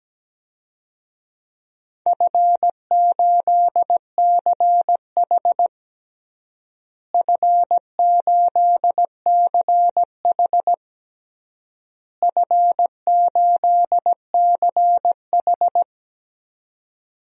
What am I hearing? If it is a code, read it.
Morse code, heard as a keyed tone, about 17 wpm, 700 Hz. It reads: F8CH F8CH F8CH
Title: DXCC-100 Call Signs - 4 Characters - Repeated 3x - 17wpm